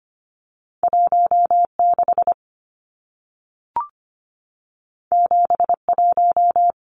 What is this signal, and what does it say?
Morse code, heard as a keyed tone, about 25 wpm, 700 Hz. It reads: 16 71
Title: Numbers - 2-Digit 25wpm